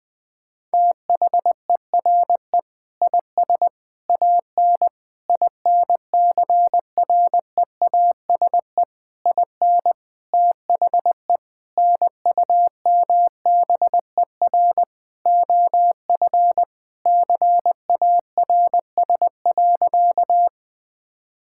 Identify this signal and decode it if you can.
Morse code, heard as a keyed tone, about 20 wpm, 700 Hz. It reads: THERE IS AN INCREASE IN THE NUMBER OF CARS.
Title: Sentences with 5-Word Phrases - Spoken Beginning - 20wpm